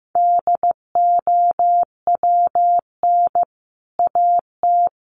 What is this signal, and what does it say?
Morse code, heard as a keyed tone, about 14 wpm, 700 Hz. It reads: DOWN AT
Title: Sentences from Top 1100 Words 15wpm